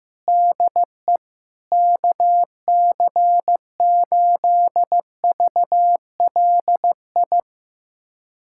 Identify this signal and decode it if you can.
Morse code, heard as a keyed tone, about 15 wpm, 700 Hz. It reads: DE KC8VLI
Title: USA Callsign DE Callsign - 6 Characters - 15wpm